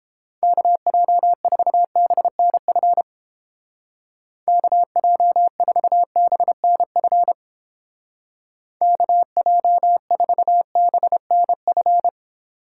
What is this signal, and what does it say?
Morse code, heard as a keyed tone, about 30 wpm, 700 Hz. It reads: KJ4BNF KJ4BNF KJ4BNF